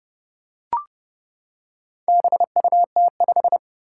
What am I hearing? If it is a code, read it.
Morse code, heard as a keyed tone, about 30 wpm, 700 Hz. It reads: BUT5
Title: Four Challenging Letter-Number Combinations 30wpm